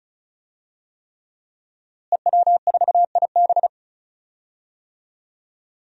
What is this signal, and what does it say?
Morse code, heard as a keyed tone, about 34 wpm, 700 Hz. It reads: EW4IB
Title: DXCC-100 Call Signs - 5 Characters - Repeated 3x - 35wpm